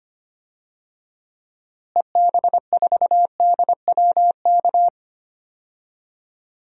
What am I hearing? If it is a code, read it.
Morse code, heard as a keyed tone, about 25 wpm, 700 Hz. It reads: EB4DWK
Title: DXCC-100 Call Signs - 6 Characters - Repeated 3x - 25wpm